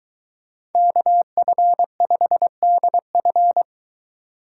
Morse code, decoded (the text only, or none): KF5DF